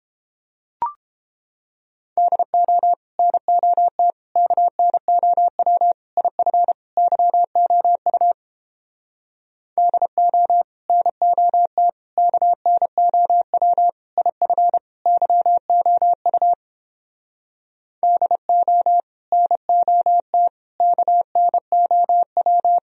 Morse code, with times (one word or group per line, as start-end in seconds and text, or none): 2.2-2.9 DO
3.2-4.1 NOT
4.4-5.9 KNOW
6.2-6.7 IF
7.0-8.3 YOU
9.8-10.6 DO
10.9-11.9 NOT
12.2-13.9 KNOW
14.2-14.8 IF
15.1-16.5 YOU
18.0-19.0 DO
19.3-20.5 NOT
20.8-22.8 KNOW